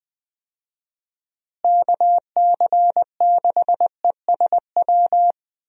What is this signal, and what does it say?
Morse code, heard as a keyed tone, about 20 wpm, 700 Hz. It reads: KC6ESW
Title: USA Call Signs - 6 Characters - Repeated 2x - 20wpm